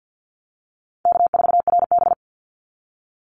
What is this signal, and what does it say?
Morse code, heard as a keyed tone, about 48 wpm, 700 Hz. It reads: K4RB
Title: Easy Call Signs - 50wpm